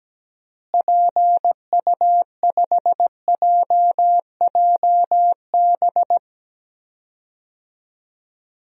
Morse code, heard as a keyed tone, about 17 wpm, 700 Hz. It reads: PU5JJB
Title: DXCC-100 Call Signs - 6 Characters - Repeated 3x - 17wpm